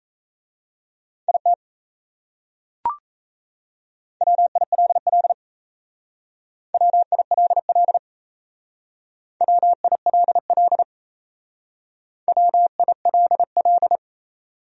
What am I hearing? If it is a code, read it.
Morse code, heard as a keyed tone, about 36 wpm, 700 Hz. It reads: IT WILL WILL WILL WILL